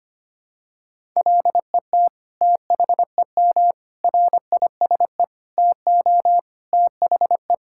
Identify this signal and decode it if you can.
Morse code, heard as a keyed tone, about 25 wpm, 700 Hz. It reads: LET THEM RISE TO THE
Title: Sentences from Top 1000 Words - Review Part 6 - 25wpm